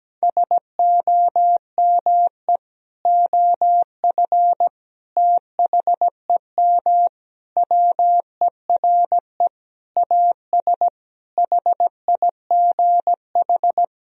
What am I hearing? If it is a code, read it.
Morse code, heard as a keyed tone, about 17 wpm, 700 Hz. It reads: SOME OF THEM WERE AS HIGH
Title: Sentences from Top 1000 Words - Review Part 1 - 17wpm